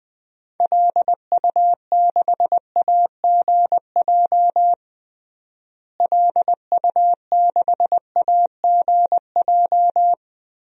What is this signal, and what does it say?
Morse code, heard as a keyed tone, about 20 wpm, 700 Hz. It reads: LU6AGJ LU6AGJ
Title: DXCC-100 Call Signs - 6 Characters - Repeated 3x - 20wpm